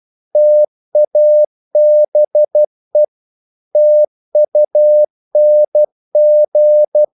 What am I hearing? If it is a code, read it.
Morse code, heard as a keyed tone, about 12 wpm, 600 Hz. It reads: TABE TUNG
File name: Korte ord DK 12wpm